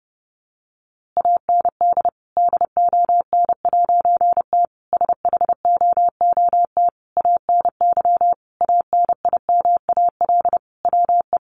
Morse code, heard as a keyed tone, about 30 wpm, 700 Hz. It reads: AND DON'T SHOOT ANY ANIMAL WE
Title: A Sound Of Thunder 30wpm